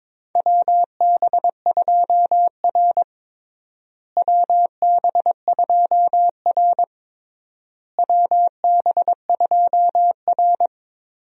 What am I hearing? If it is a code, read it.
Morse code, heard as a keyed tone, about 22 wpm, 700 Hz. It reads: WB2R WB2R WB2R
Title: USA Call Signs - 4 Characters - Repeated 3x - 22wpm